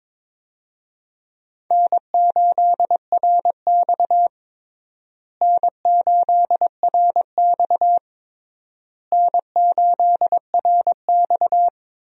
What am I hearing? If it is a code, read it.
Morse code, heard as a keyed tone, about 22 wpm, 700 Hz. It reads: N8RX N8RX N8RX
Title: USA Call Signs - 4 Characters - Repeated 3x - 22wpm